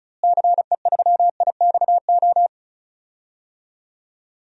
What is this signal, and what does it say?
Morse code, heard as a keyed tone, about 36 wpm, 700 Hz. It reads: CE3IXO
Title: DXCC-100 Call Signs - 6 Characters - Repeated 2x - 35wpm